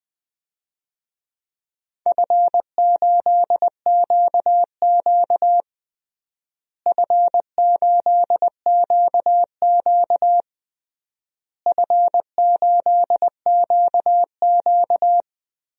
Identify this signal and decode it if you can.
Morse code, heard as a keyed tone, about 20 wpm, 690 Hz. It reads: F8QQ F8QQ F8QQ
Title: DXCC-100 Call Signs - 4 Characters - Repeated 3x - 20wpm